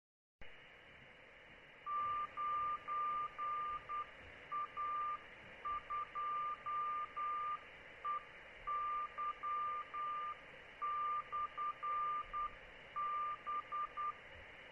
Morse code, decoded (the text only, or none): 9A2EY/B